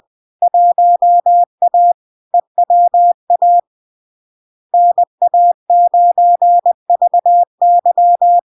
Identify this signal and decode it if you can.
Morse code, heard as a keyed tone, about 20 wpm, 700 Hz. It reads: 1A EWA NA9VY
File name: LARK-FD-2022-Log-in-CW-20wpm-700-Hz